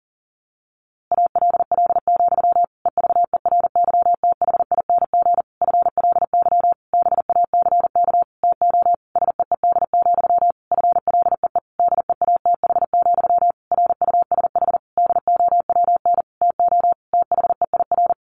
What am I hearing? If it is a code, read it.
Morse code, heard as a keyed tone, about 40 wpm, 700 Hz. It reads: ALL, EVERYTHING FLY BACK TO SEED, FLEE DEATH, RUSH DOWN TO THEIR
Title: A Sound Of Thunder 40wpm